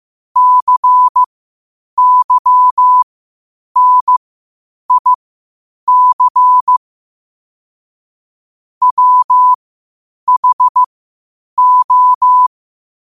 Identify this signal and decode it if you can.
Morse code, heard as a keyed tone, about 15 wpm, spaced as for 8 wpm, 1000 Hz. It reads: CYNIC WHO
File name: qotdfiles / 8wpm